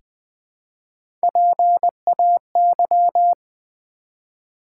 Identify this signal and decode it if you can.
Morse code, heard as a keyed tone, about 20 wpm, 700 Hz. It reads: PAY
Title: Three Letter Words - 20wpm